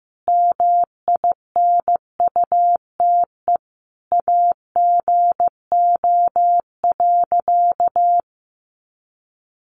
Morse code, heard as a keyed tone, about 14 wpm, 700 Hz. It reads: MINUTE AGO.
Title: Sentences from Top 500 Words 15wpm